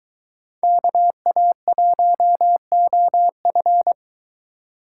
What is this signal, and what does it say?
Morse code, heard as a keyed tone, about 23 wpm, 700 Hz. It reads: KA1OF